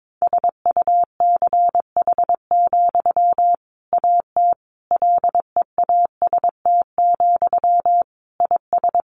Morse code, heard as a keyed tone, about 22 wpm, 700 Hz. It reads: SUCH, AT LEAST, IS
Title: Sentences from Top 1500 Words 22wpm